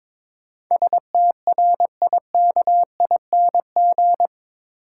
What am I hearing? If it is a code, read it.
Morse code, heard as a keyed tone, about 22 wpm, 700 Hz. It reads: STRIKING
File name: practice / 22wpm / Eight Letter Words - 22wpm